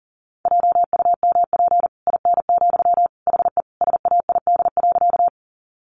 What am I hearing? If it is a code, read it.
Morse code, heard as a keyed tone, about 40 wpm, 700 Hz. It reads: JUMP IN, HE SAID.
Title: Sentences from Top 1300 Words 40wpm